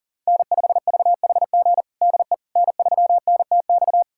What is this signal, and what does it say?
Morse code, heard as a keyed tone, about 40 wpm, 700 Hz. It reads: N5VHG DE N3NTX
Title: USA Callsign DE Callsign - 5 Characters - 40wpm